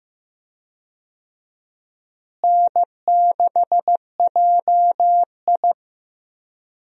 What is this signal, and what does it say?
Morse code, heard as a keyed tone, about 15 wpm, 700 Hz. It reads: N6JI
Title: USA Call Signs - 4 Characters - Repeated 3x - 15wpm